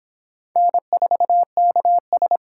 Morse code, heard as a keyed tone, about 28 wpm, 700 Hz. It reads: N4KS